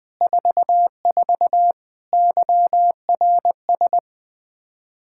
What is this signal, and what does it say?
Morse code, heard as a keyed tone, about 20 wpm, 700 Hz. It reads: 44 YRS
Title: Practice QSO - Exchange 3 - More Info - 20wpm